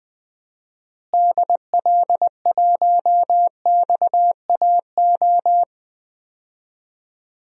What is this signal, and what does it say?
Morse code, heard as a keyed tone, about 20 wpm, 700 Hz. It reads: DL1XAO
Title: DXCC-100 Call Signs - 6 Characters - Repeated 2x - 20wpm